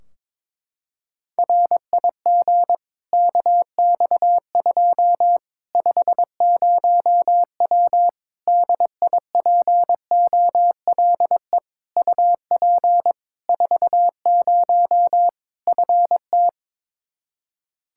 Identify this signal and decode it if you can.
Morse code, heard as a keyed tone, about 22 wpm, 700 Hz. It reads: RIG KX2 50W DIPOLE UP 40 FT
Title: Practice QSO - Exchange 2 - Info - 22wpm